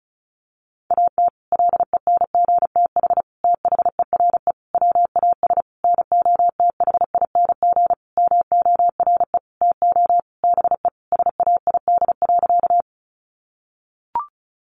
Morse code, heard as a keyed tone, about 34 wpm, 700 Hz. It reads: AT LENGTH THERE WAS NOTHING MORE TO BE SAID.
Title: Sentences from Top 1000 Words 35wpm